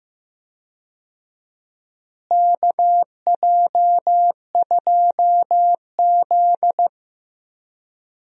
Morse code, heard as a keyed tone, about 15 wpm, 700 Hz. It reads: KJ2Z
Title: USA Call Signs - 4 Characters - Repeated 2x - 15wpm